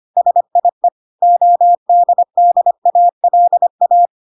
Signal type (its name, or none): Morse code